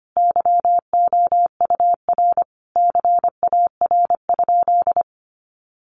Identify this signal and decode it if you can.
Morse code, heard as a keyed tone, about 25 wpm, 700 Hz. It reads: YOUR CAR?
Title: Sentences from Top 1500 Words 25wpm